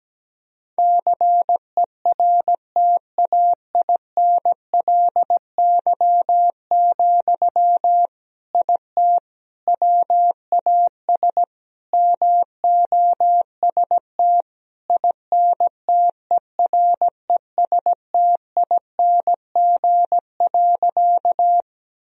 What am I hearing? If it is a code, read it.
Morse code, heard as a keyed tone, about 17 wpm, 700 Hz. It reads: CERTAINLY, IT WAS MOST INTERESTING.
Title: Sentences from Top 1000 Words - Review Part 1 - 17wpm